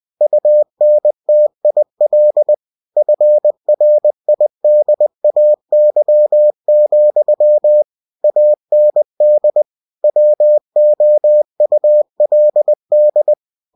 Morse code, heard as a keyed tone, about 20 wpm, 600 Hz. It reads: UNTIL FRIDAY, AND WOULD
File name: cw-books / winnie-the-pooh / winnie-20wpm-page-0022